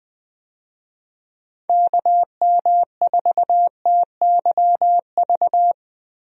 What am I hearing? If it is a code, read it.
Morse code, heard as a keyed tone, about 20 wpm, 700 Hz. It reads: KM4TYV
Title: USA Call Signs - 6 Characters - Repeated 3x - 20wpm